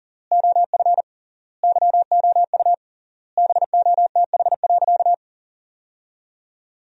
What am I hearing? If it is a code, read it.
Morse code, heard as a keyed tone, about 40 wpm, 700 Hz. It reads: OF YOU BOTH.
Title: 3x Word-Spacing - Sentences from Top 300 Words - 40wpm